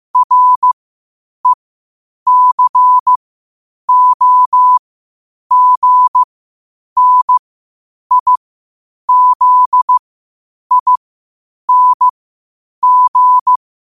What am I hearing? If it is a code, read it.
Morse code, heard as a keyed tone, about 15 wpm, spaced as for 8 wpm, 1000 Hz. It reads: RECOGNIZING
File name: qotdfiles / 8wpm